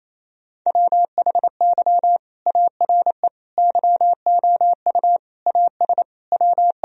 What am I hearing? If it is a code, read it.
Morse code, heard as a keyed tone, about 27 wpm, 700 Hz. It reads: WHY ARE YOU AS W
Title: Sentences from Top 1000 Words - Review Part 3 - 28wpm